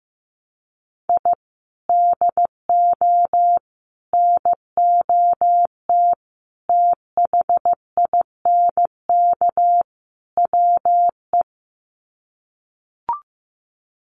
Morse code, heard as a keyed tone, about 15 wpm, 700 Hz. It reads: I DO NOT THINK WE
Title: Sets of 5 Words 15wpm